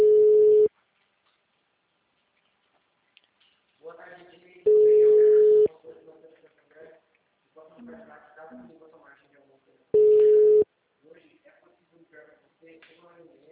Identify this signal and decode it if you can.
Ringback tone (Europe, 425 Hz), with speech.
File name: out- -2011- - - .17487